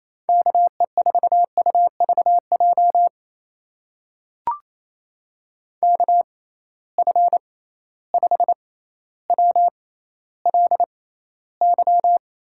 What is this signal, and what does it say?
Morse code, heard as a keyed tone, about 28 wpm, spaced as for 14 wpm, 700 Hz. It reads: KE4UVJ KF5WLY
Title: USA Call Signs - 6 Characters - 6x Character Spacing - 28wpm